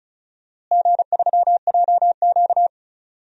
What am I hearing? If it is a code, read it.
Morse code, heard as a keyed tone, about 36 wpm, 700 Hz. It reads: G3JQ
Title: DXCC-100 Call Signs - 4 Characters - Repeated 3x - 35wpm